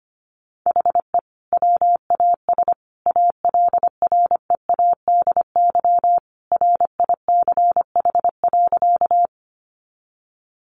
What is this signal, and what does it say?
Morse code, heard as a keyed tone, about 25 wpm, 700 Hz. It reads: HE WAS ALREADY RICH.
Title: Sentences from Top 1100 Words 25wpm